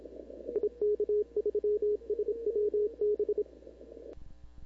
Morse code, heard as a keyed tone, about 27 wpm, 420 Hz. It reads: IK3SWB
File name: CW / 2005-k5tr arrl cw dx -16:00